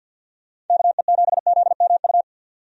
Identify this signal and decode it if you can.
Morse code, heard as a keyed tone, about 52 wpm, 700 Hz. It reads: KE7ZMU
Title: USA Call Signs - 6 Characters - 50wpm